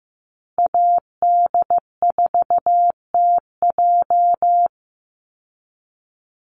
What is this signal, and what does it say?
Morse code, heard as a keyed tone, about 15 wpm, 700 Hz. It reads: AD4TJ
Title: Call Signs Repeated 3x - 15wpm